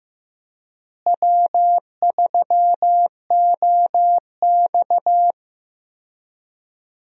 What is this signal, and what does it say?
Morse code, heard as a keyed tone, about 16 wpm, 700 Hz. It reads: W3OX
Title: USA Call Signs - 4 Characters - Repeated 2x - 15wpm